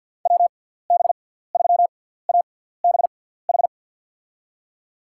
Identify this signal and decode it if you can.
Morse code, heard as a keyed tone, about 53 wpm, spaced as for 17 wpm, 700 Hz. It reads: WB3ABH
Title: USA Call Signs - 6 Characters - 6x Character Spacing - 50wpm